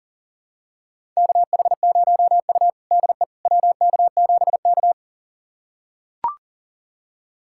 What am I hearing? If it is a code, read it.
Morse code, heard as a keyed tone, about 40 wpm, 700 Hz. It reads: KH0U DE WK7K
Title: USA Callsign DE Callsign - 4 Characters - 40wpm